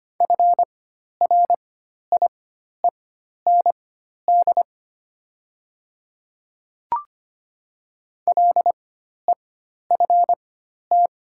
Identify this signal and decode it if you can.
Morse code, heard as a keyed tone, about 25 wpm, spaced as for 12 wpm, 700 Hz. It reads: FRIEND LEFT
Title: 4x Character Spacing - Top 300 Words - 25wpm